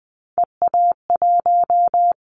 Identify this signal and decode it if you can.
Morse code, heard as a keyed tone, about 19 wpm, 700 Hz. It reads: EA1
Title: Callsign Prefixes for Top 25 DXCC - 20wpm